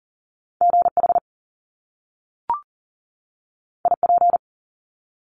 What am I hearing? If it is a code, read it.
Morse code, heard as a keyed tone, about 39 wpm, 700 Hz. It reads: GH IP
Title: Top Two Letter Combinations in Words 40wpm